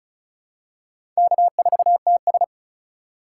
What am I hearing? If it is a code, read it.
Morse code, heard as a keyed tone, about 36 wpm, 700 Hz. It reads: K4TS
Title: USA Call Signs - 4 Characters - 2x Character Spacing - 35wpm